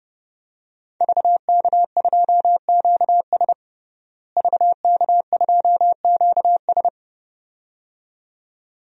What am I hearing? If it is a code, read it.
Morse code, heard as a keyed tone, about 30 wpm, 700 Hz. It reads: VK2QS VK2QS
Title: DXCC-100 Call Signs - 5 Characters - Repeated 2x - 30wpm